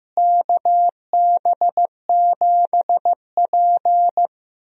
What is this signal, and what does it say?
Morse code, heard as a keyed tone, about 15 wpm, 700 Hz. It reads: KB7P
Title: USA Call Signs - 4 Characters - Repeated 3x - 15wpm